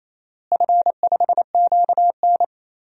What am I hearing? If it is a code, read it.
Morse code, heard as a keyed tone, about 28 wpm, 700 Hz. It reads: F5QN
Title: DXCC-100 Call Signs - 4 Characters - Repeated 3x - 28wpm